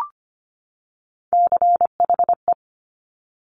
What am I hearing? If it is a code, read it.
Morse code, heard as a keyed tone, about 25 wpm, 700 Hz. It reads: CHE
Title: Top Three Letter Combinations in Words 25wpm